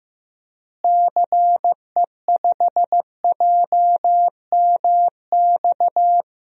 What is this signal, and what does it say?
Morse code, heard as a keyed tone, about 15 wpm, 700 Hz. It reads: CE5JMX
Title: DXCC-100 Call Signs - 6 Characters - Repeated 2x - 15wpm